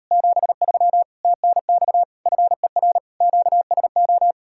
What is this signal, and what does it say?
Morse code, heard as a keyed tone, about 39 wpm, 700 Hz. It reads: 73 TNX FER QSO